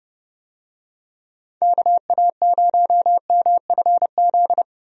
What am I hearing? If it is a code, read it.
Morse code, heard as a keyed tone, about 30 wpm, 700 Hz. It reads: KA0MFZ
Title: USA Call Signs - 6 Characters - Repeated 3x - 30wpm